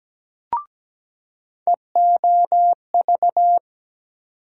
Morse code, heard as a keyed tone, about 17 wpm, 700 Hz. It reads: EOV
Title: USA Call Sign Postfixes - 3 Letters 17wpm